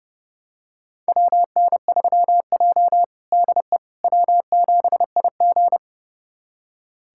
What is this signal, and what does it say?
Morse code, heard as a keyed tone, about 30 wpm, 700 Hz. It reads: WN3J DE W7IG
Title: USA Callsign DE Callsign - 4 Characters - 30wpm